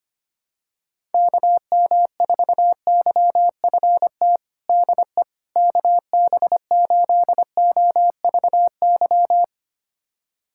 Morse code, heard as a keyed tone, about 25 wpm, 700 Hz. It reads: KM4YFT DE KB8OVY
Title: USA Callsign DE Callsign - 6 Characters - 25wpm